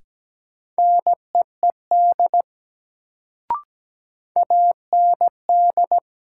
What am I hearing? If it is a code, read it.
Morse code, heard as a keyed tone, about 17 wpm, 700 Hz. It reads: NEED AND
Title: Sets of 2 Words - New Word Drill - 17wpm